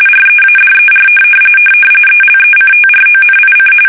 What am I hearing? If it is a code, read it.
RTTY, about 45.45 baud. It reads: READER ELECTRONIC
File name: fsk850 / teletype-RTTY